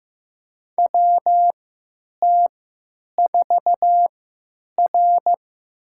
Morse code, heard as a keyed tone, about 15 wpm, spaced as for 9 wpm, 700 Hz. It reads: WT4R
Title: USA Call Signs - 4 Characters - 3x Character Spacing - 15wpm